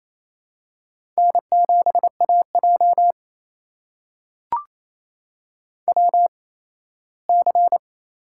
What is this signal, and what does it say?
Morse code, heard as a keyed tone, about 28 wpm, spaced as for 17 wpm, 700 Hz. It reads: N7AJ WC
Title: USA Call Signs - 4 Characters - 8x Character Spacing - 28wpm